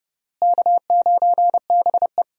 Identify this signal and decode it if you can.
Morse code, heard as a keyed tone, about 30 wpm, 700 Hz. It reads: K9BE
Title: USA Call Signs - 4 Characters - Repeated 3x - 30wpm